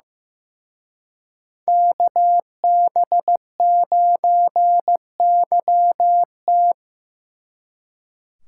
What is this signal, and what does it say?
Morse code, heard as a keyed tone, about 18 wpm, 700 Hz. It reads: KB9YT
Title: Speed Racing - USA 2x2 Call Signs 15wpm